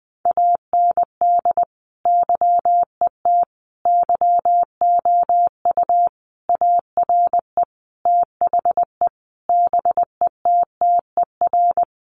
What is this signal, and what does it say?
Morse code, heard as a keyed tone, about 20 wpm, 700 Hz. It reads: AND YET YOU ARE THE BETTER
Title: Sentences from Top 500 Words 20wpm